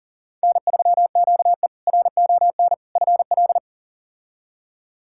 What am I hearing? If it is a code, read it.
Morse code, heard as a keyed tone, about 37 wpm, 700 Hz. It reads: N3QE RON FL
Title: K1USN SST QSO Practice - 40wpm